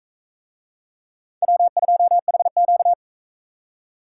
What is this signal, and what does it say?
Morse code, heard as a keyed tone, about 38 wpm, 700 Hz. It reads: W2HQ